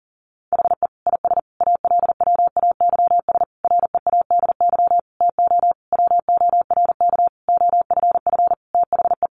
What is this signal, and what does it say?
Morse code, heard as a keyed tone, about 40 wpm, 700 Hz. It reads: HE IS ALWAYS READY TO WORK OFF THE
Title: Sentences from Top 1300 Words 40wpm